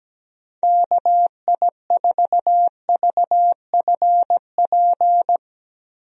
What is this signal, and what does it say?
Morse code, heard as a keyed tone, about 17 wpm, 700 Hz. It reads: KI4VFP
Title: USA Call Signs - 6 Characters - 17wpm